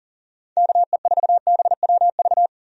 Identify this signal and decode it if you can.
Morse code, heard as a keyed tone, about 37 wpm, 700 Hz. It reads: KE4BWV
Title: USA Call Signs - 6 Characters - 40wpm